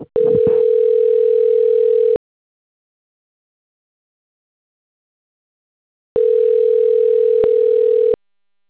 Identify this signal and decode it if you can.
Ringback tone (North America), 440 + 480 Hz.